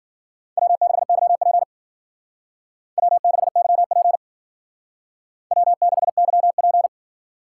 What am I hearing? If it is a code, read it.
Morse code, heard as a keyed tone, about 51 wpm, 700 Hz. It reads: W6YP W6YP W6YP